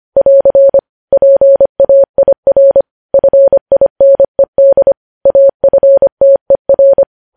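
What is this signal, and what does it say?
Morse code, heard as a keyed tone, about 25 wpm, 550 Hz. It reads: <AR> PAIR FINED AFTER
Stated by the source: News headlines sent in Morse at 25 wpm.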